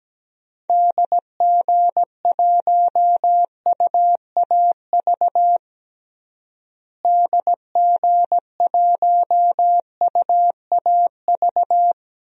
Morse code, heard as a keyed tone, about 17 wpm, 700 Hz. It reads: DG1UAV DG1UAV
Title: DXCC-100 Call Signs - 6 Characters - Repeated 2x - 17wpm